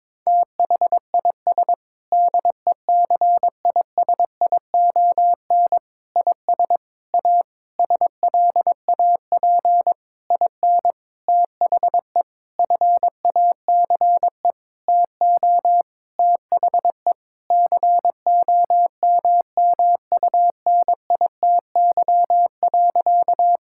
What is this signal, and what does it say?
Morse code, heard as a keyed tone, about 22 wpm, 700 Hz. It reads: THIS DECISION IS A SLAP IN THE FACE TO THE COMMUNITY.
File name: practice / 22wpm / Sentences with 8-Word Phrases - 22wpm